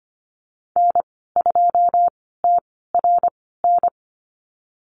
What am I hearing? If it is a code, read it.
Morse code, heard as a keyed tone, about 25 wpm, spaced as for 16 wpm, 700 Hz. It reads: N2TRN
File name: Lesson 15g - Number 5 - Call Signs 25wpm